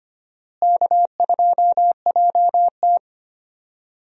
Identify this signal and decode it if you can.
Morse code, heard as a keyed tone, about 25 wpm, 700 Hz. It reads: K2JT
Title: USA Call Signs - 4 Characters - Repeated 3x - 25wpm